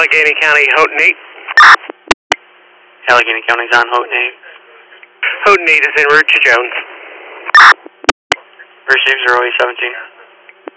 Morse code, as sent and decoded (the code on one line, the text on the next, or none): none